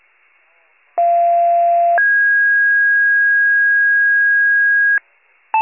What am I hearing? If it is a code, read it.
Two-tone page, heard as a one-second tone, then a three-second tone, about 690 then 1650 Hz. It is heard from 1.0 to 5.0 s.
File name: pager / 41 RESCUE DUTY / wcalert